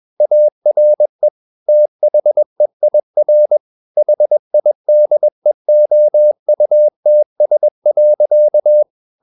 Morse code, heard as a keyed tone, about 21 wpm, 600 Hz. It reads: ARE THEIR HIDEOUTS.
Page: CW med Gnister